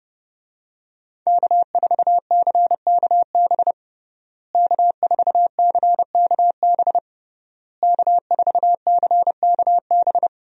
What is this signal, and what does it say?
Morse code, heard as a keyed tone, about 30 wpm, 700 Hz. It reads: K4CKB K4CKB K4CKB
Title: USA Call Signs - 5 Characters - Repeated 3x - 30wpm